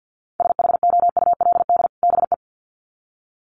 Morse code, heard as a keyed tone, about 50 wpm, 700 Hz. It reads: SHOULD BE
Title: Sets of 2 Words 50wpm